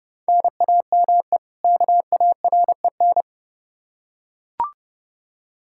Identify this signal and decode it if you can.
Morse code, heard as a keyed tone, about 30 wpm, 700 Hz. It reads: NAME KAREN
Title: Practice QSO - Exchange 1 - Beginning - 30wpm